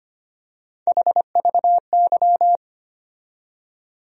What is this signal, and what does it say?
Morse code, heard as a keyed tone, about 25 wpm, 700 Hz. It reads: HVY
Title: USA Call Sign Postfixes - 3 Letters 25wpm